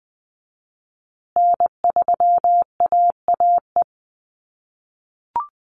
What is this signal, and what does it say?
Morse code, heard as a keyed tone, about 20 wpm, 700 Hz. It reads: N3AAE
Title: USA Call Signs Repeated 3x 20wpm